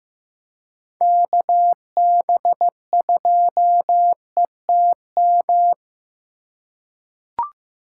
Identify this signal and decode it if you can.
Morse code, heard as a keyed tone, about 15 wpm, 700 Hz. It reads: KB2ETM
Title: USA Call Signs - 6 Characters - 4x Character Spacing - 15wpm